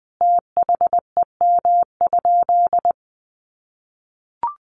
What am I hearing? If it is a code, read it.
Morse code, heard as a keyed tone, about 20 wpm, 700 Hz. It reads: THEM?
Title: Sentences from Top 200 Words 20wpm